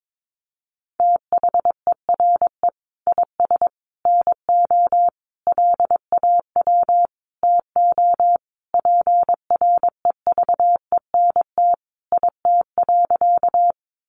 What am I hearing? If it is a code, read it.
Morse code, heard as a keyed tone, about 22 wpm, 700 Hz. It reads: THERE IS NO LAW TO PREVENT IT.
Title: Sentences from Top 1100 Words 22wpm